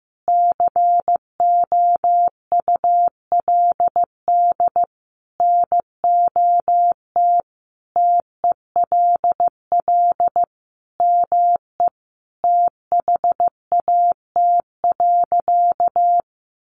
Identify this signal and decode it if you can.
Morse code, heard as a keyed tone, about 15 wpm, 700 Hz. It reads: COULD NOT TELL ME THAT.
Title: Sentences from Top 600 Words 15wpm